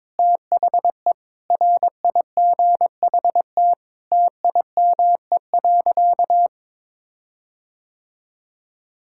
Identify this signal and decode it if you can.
Morse code, heard as a keyed tone, about 22 wpm, 700 Hz. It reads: THE RIGHT TIME.
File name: Sentences with 7-Word Phrases - 22wpm